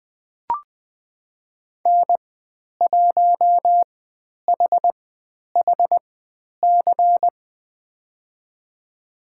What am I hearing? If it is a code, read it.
Morse code, heard as a keyed tone, about 20 wpm, spaced as for 10 wpm, 700 Hz. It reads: N1HHC
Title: 109 Lesson 18g - Letter m - Call Signs 20wpm